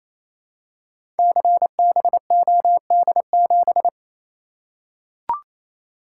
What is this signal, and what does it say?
Morse code, heard as a keyed tone, about 27 wpm, 700 Hz. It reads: CBOD7
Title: Code Groups - 5 Letter:Numbers - 28wpm